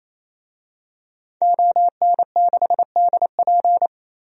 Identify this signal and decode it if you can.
Morse code, heard as a keyed tone, about 28 wpm, 700 Hz. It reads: ON6DP
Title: DXCC-100 Call Signs - 5 Characters - 28wpm